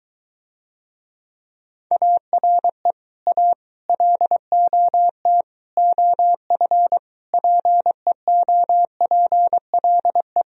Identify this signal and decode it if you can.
Morse code, heard as a keyed tone, about 23 wpm, 700 Hz. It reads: ARE A LOT OF PEOPLE